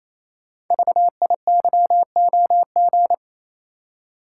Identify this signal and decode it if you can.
Morse code, heard as a keyed tone, about 27 wpm, 700 Hz. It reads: VIYOG
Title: Code Groups - 5 Letters - 28wpm